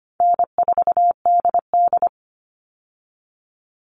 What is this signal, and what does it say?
Morse code, heard as a keyed tone, about 25 wpm, 700 Hz. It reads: N4DD
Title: All Cut Numbers - 4-Digit 25wpm